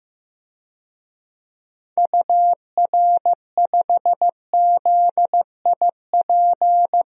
Morse code, heard as a keyed tone, about 15 wpm, 700 Hz. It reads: UR5ZIP
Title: DXCC-100 Call Signs - 6 Characters - Repeated 3x - 15wpm